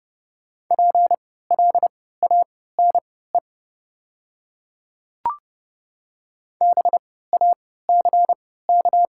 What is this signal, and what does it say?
Morse code, heard as a keyed tone, about 30 wpm, spaced as for 17 wpm, 700 Hz. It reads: PLANE BACK
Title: 3x Character Spacing - Top 500 Words - 30wpm